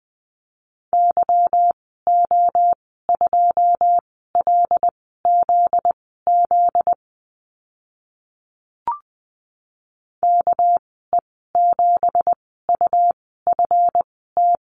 Morse code, heard as a keyed tone, about 20 wpm, spaced as for 14 wpm, 700 Hz. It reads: YO2LZZ KE7UFT
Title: 245 Lesson 39g - Letter z - Call Signs 20wpm